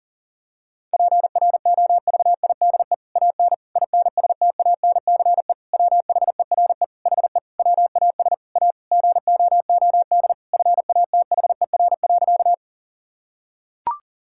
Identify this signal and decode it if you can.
Morse code, heard as a keyed tone, about 37 wpm, 700 Hz. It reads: PROVIDE AN INSTANCE WHERE HE WAS A GOOD FATHER.
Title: Sentences from Top 1000 Words - Review Part 5 - 40wpm